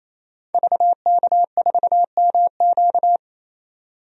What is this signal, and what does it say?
Morse code, heard as a keyed tone, about 28 wpm, 690 Hz. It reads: VK4MQ
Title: DXCC-100 Call Signs - 5 Characters - 28wpm